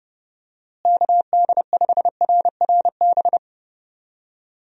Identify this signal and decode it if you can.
Morse code, heard as a keyed tone, about 30 wpm, 700 Hz. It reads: KD5RRB
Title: USA Call Signs - 6 Characters - Repeated 2x - 30wpm